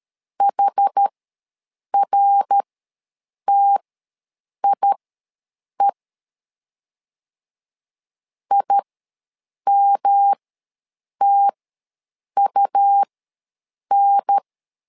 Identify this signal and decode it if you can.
Morse code, heard as a keyed tone, about 13 wpm, spaced as for 7 wpm, 800 Hz. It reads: HRTIE IMTUN
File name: LEZIONI / 3 MWRO / 3 MWR ws 12wpm 1809